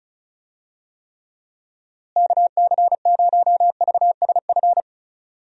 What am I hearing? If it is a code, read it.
Morse code, heard as a keyed tone, about 34 wpm, 700 Hz. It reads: KC0VSF
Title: USA Call Signs - 6 Characters - Repeated 2x - 35wpm